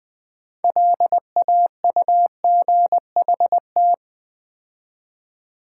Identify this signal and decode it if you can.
Morse code, heard as a keyed tone, about 20 wpm, 700 Hz. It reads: LAUGHT
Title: Build A Word - 2 Letters At A Time - 8-Letter Words - 20wpm